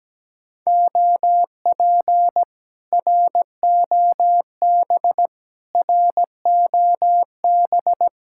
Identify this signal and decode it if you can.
Morse code, heard as a keyed tone, about 17 wpm, 700 Hz. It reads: OP ROB ROB
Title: Speed-Racing Common QSO Phrases Encore - 17wpm